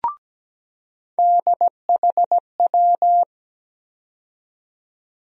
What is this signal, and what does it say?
Morse code, heard as a keyed tone, about 17 wpm, 700 Hz. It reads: DHW
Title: USA Call Sign Postfixes - 3 Letters 17wpm